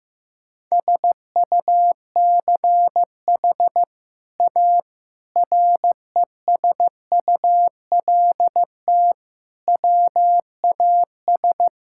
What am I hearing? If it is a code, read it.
Morse code, heard as a keyed tone, about 16 wpm, 700 Hz. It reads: SUCH A RESULT WAS
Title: Sentences from Top 1000 Words - Review Part 5 - 15wpm